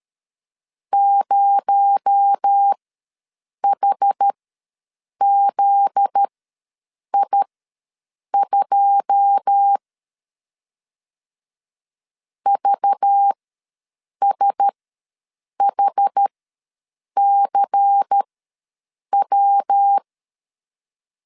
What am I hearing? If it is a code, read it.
Morse code, heard as a keyed tone, about 13 wpm, spaced as for 7 wpm, 800 Hz. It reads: 0HZI2 VSHCW
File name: LEZIONI / 9 2 ws 12wpm 3406